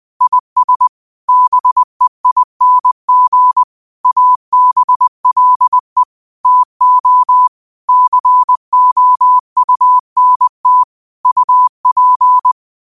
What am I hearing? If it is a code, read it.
Morse code, heard as a keyed tone, about 20 wpm, 1000 Hz. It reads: IS BEING ABLE TO COUNT UP
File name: qotdfiles / 20wpm